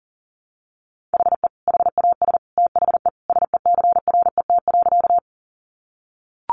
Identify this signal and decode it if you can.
Morse code, heard as a keyed tone, about 40 wpm, 700 Hz. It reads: HE HAS THE SECRET.
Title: Sentences from Top 1200 Words 40wpm